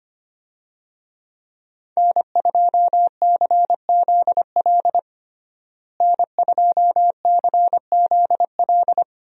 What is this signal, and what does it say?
Morse code, heard as a keyed tone, about 25 wpm, 700 Hz. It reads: N2CZL N2CZL
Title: USA Call Signs - 5 Characters - Repeated 2x - 25wpm